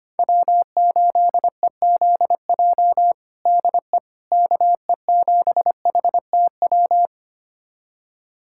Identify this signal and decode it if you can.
Morse code, heard as a keyed tone, about 25 wpm, 700 Hz. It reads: W8EZJ DE KE7HTW
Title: USA Callsign DE Callsign 25wpm